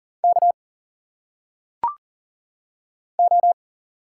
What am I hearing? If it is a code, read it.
Morse code, heard as a keyed tone, about 40 wpm, 700 Hz. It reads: K O
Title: Single Letters - Phonetic Pronunciation 40wpm